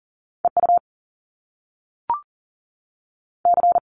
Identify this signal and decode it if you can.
Morse code, heard as a keyed tone, about 42 wpm, spaced as for 3 wpm, 700 Hz. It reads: E UC
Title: Top Two Letter Combinations in Words 40wpm